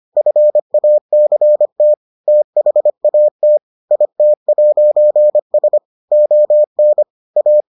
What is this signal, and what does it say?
Morse code, heard as a keyed tone, about 25 wpm, 600 Hz. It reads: FACT THAT IT'S ON A